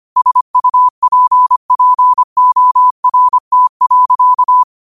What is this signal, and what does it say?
Morse code, heard as a keyed tone, about 25 wpm, 1000 Hz. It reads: SUPPORT.